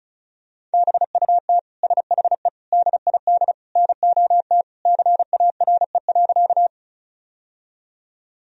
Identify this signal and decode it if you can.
Morse code, heard as a keyed tone, about 36 wpm, 700 Hz. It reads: BUT SHE DID NOT CARE.
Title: Sentences from Top 1000 Words - Review Part 1 - 35wpm